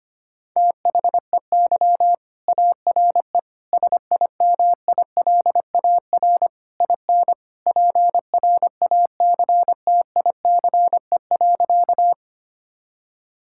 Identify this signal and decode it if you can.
Morse code, heard as a keyed tone, about 25 wpm, 700 Hz. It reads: THEY ARE SIMILAR IN PRACTICE.
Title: Sentences from Top 1000 Words - Review Part 5 - 25wpm